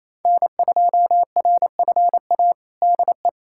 Morse code, heard as a keyed tone, about 27 wpm, 700 Hz. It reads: N2RFA DE
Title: USA Callsign DE Callsign - 5 Characters - 28wpm